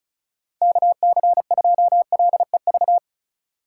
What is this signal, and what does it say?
Morse code, heard as a keyed tone, about 36 wpm, 700 Hz. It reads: KC2LEV
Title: USA Call Signs - 6 Characters - 6x Character Spacing - 35wpm